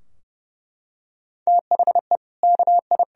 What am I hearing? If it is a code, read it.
Morse code, heard as a keyed tone, about 30 wpm, 700 Hz. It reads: THE KI
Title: Sentences from Top 1000 Words - Review Part 2 - 30wpm